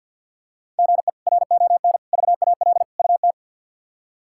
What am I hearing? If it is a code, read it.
Morse code, heard as a keyed tone, about 52 wpm, 700 Hz. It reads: GE RON VAL UT
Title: K1USN SST QSO Practice - 50wpm